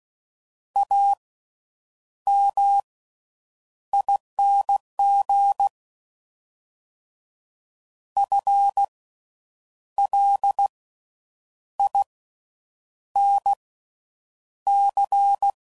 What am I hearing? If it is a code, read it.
Morse code, heard as a keyed tone, about 16 wpm, spaced as for 7 wpm, 800 Hz. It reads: AMING FLINC
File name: Slow With Faster Ings